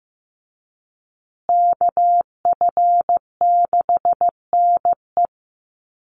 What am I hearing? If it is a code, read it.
Morse code, heard as a keyed tone, about 15 wpm, 700 Hz. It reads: KF6NE
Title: USA Call Signs Repeated 3x 15wpm